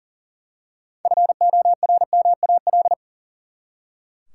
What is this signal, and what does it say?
Morse code, heard as a keyed tone, about 40 wpm, 700 Hz. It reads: FORMAL
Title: Six Letter Words - 40wpm